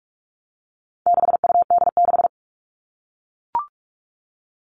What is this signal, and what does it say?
Morse code, heard as a keyed tone, about 44 wpm, 700 Hz. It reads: 6UD6
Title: All Cut Numbers - 4-Digit 45wpm